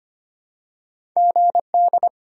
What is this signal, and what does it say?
Morse code, heard as a keyed tone, about 25 wpm, 700 Hz. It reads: GD
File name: Top 90 QSO Elements - 4x Character Spacing - 25wpm